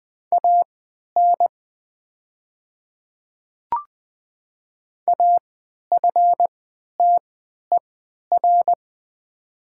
Morse code, heard as a keyed tone, about 20 wpm, spaced as for 11 wpm, 700 Hz. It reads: AN AFTER
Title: 3x Character Spacing - Top 300 Words - 20wpm